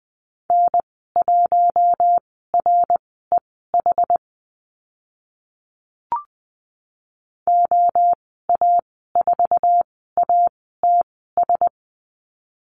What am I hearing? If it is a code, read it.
Morse code, heard as a keyed tone, about 20 wpm, spaced as for 14 wpm, 700 Hz. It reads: N1REH OA4ATS
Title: 082 Lesson 14g - Number 2 - Call Signs 20wpm